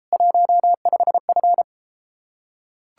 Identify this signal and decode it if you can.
Morse code, heard as a keyed tone, about 33 wpm, 700 Hz. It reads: 15F